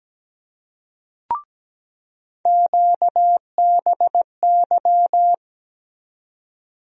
Morse code, heard as a keyed tone, about 17 wpm, 700 Hz. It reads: QBY
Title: USA Call Sign Postfixes - 3 Letters 17wpm